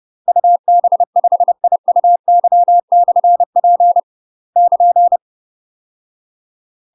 Morse code, heard as a keyed tone, about 30 wpm, 700 Hz. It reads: UB5IUY/P <KN>
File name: UK MASTER QSOs TRAINING/30WPM